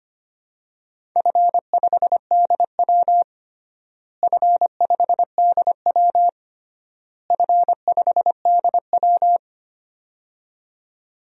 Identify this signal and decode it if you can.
Morse code, heard as a keyed tone, about 25 wpm, 700 Hz. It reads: F5DW F5DW F5DW
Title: DXCC-100 Call Signs - 4 Characters - Repeated 3x - 25wpm